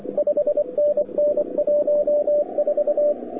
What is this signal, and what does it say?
Morse code, heard as a keyed tone, about 24 wpm, 590 Hz, with radio noise. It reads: 5NN 14
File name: wwcw08 / k5zd wwcw08 1300z